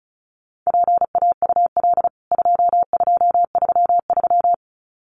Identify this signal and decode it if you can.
Morse code, heard as a keyed tone, about 34 wpm, 700 Hz. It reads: PAUL 2233
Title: CWT Exchange 35wpm